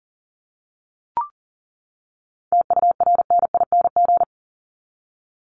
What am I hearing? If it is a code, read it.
Morse code, heard as a keyed tone, about 40 wpm, 700 Hz. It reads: TURNING
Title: ING Suffix - Top 100 Words with Base Words 40wpm